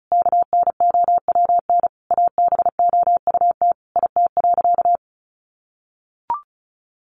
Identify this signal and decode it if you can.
Morse code, heard as a keyed tone, about 34 wpm, 700 Hz. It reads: KNOWN ABOUT IT.
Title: Sentences from Top 1200 Words 35wpm